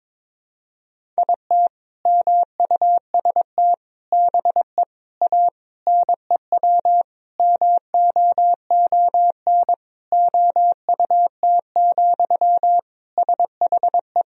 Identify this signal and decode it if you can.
Morse code, heard as a keyed tone, about 22 wpm, 700 Hz. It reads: IT MUST BE A NEW MOON OUT, SHE
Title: Sentences from Top 500 Words and Abbreviations 22wpm